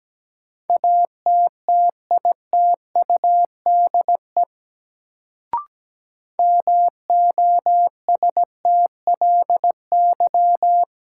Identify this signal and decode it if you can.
Morse code, heard as a keyed tone, about 17 wpm, 700 Hz. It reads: ATTITUDE MOSTLY
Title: Sentences from Top 1300 Words - New Word Drill - 17wpm